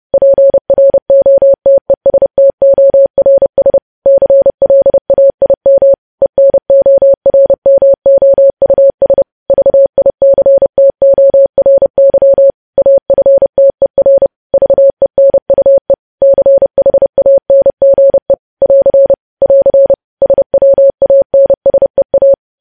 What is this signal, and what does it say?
Morse code, heard as a keyed tone, about 30 wpm, 550 Hz. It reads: PROTESTORS CLAIM ENORMOUS VICTORY AFTER VENUE CHANGE <AR> <AR> SWANSEA
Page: News Headlines in Morse Code at 30 WPM